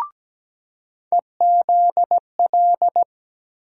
Morse code, heard as a keyed tone, about 17 wpm, 700 Hz. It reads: EZL
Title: USA Call Sign Postfixes - 3 Letters 17wpm